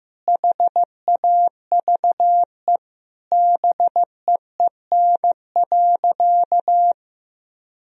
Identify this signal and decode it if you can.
Morse code, heard as a keyed tone, about 15 wpm, 700 Hz. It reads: HAVE BEEN.
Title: Sentences from Top 1000 Words - Review Part 4 - 15wpm